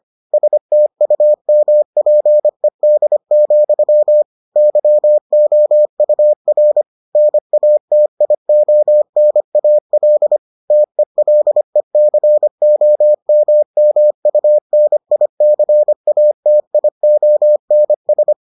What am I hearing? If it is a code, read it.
Morse code, heard as a keyed tone, about 25 wpm, 600 Hz. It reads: STUMPED, YOUR NATIONAL TELECOMMUNICATIONS